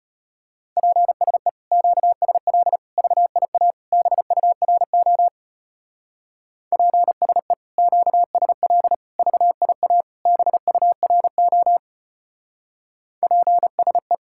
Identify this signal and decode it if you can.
Morse code, heard as a keyed tone, about 29 wpm, 700 Hz. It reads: PSE QSL VIA BURO PSE QSL VIA BURO PSE
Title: Speed-Racing Common QSO Phrases Encore - 25wpm